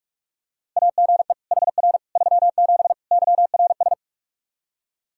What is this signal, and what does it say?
Morse code, heard as a keyed tone, about 42 wpm, 700 Hz. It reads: AGE HR 37 YRS
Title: Practice QSO - Exchange 3 - More Info - 45wpm